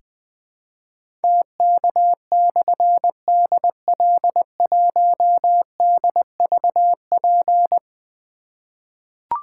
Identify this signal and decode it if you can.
Morse code, heard as a keyed tone, about 20 wpm, 700 Hz. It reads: TK/DL1DVP
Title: Difficult Call Signs 20wpm